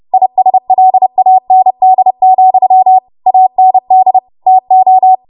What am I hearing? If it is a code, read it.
Morse code, heard as a keyed tone, about 30 wpm, 750 Hz. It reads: ISLAND, AND TO
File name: Morse / 30WPM